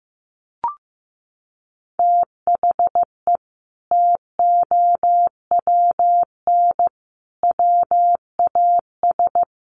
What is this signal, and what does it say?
Morse code, heard as a keyed tone, about 15 wpm, 700 Hz. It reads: THE TOWN WAS
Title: Sentences from Top 1300 Words 15wpm